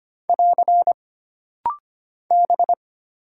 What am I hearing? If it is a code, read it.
Morse code, heard as a keyed tone, about 25 wpm, 700 Hz. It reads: <AR> B
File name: Single Character (Letter:Number:Punctuation:Prosign) - Rapid-Fire - 25wpm